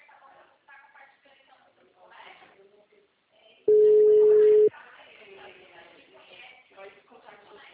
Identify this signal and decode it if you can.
Ringback tone (Europe, 425 Hz), with speech.
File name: out- -2017